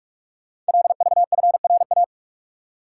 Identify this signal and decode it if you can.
Morse code, heard as a keyed tone, about 44 wpm, 700 Hz. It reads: LVFRA
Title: Code Groups - 5 Letters - 45wpm